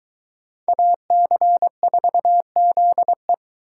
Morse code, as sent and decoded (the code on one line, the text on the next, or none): .- -.-. ....- --.. .
AC4ZE